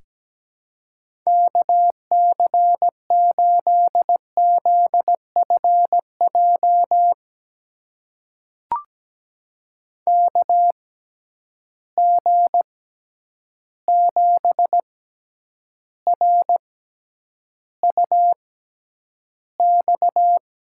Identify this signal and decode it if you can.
Morse code, heard as a keyed tone, about 17 wpm, spaced as for 9 wpm, 700 Hz. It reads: KC8ZFJ KG7RUX
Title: USA Call Signs - 6 Characters - 6x Character Spacing - 17wpm